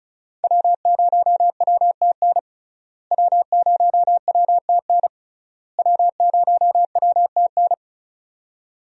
Morse code, heard as a keyed tone, about 34 wpm, 700 Hz. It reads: W0WTN W0WTN W0WTN
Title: USA Call Signs - 5 Characters - Repeated 3x - 35wpm